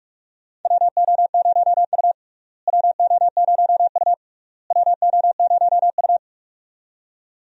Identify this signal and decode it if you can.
Morse code, heard as a keyed tone, about 46 wpm, 700 Hz. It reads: WO0U WO0U WO0U
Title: USA Call Signs - 4 Characters - Repeated 3x - 45wpm